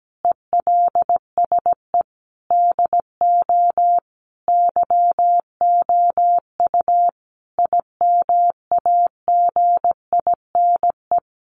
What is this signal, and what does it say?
Morse code, heard as a keyed tone, about 17 wpm, 700 Hz. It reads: ELSE DO YOU IMAGINE
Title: Sentences from Top 1400 Words 17wpm